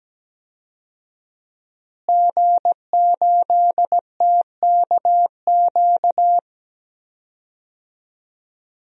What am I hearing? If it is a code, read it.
Morse code, heard as a keyed tone, about 17 wpm, 700 Hz. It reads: G8TKQ
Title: DXCC-100 Call Signs - 5 Characters - Repeated 2x - 17wpm